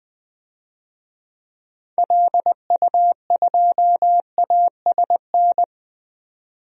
Morse code, heard as a keyed tone, about 20 wpm, 700 Hz. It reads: LU2ASN
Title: DXCC-100 Call Signs - 6 Characters - 20wpm